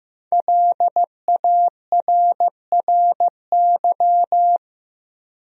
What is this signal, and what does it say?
Morse code, heard as a keyed tone, about 15 wpm, 700 Hz. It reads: LARRY
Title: USA Common Names - 15wpm